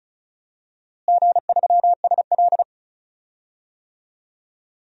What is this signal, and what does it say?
Morse code, heard as a keyed tone, about 36 wpm, 700 Hz. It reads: G3SL
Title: DXCC-100 Call Signs - 4 Characters - Repeated 2x - 35wpm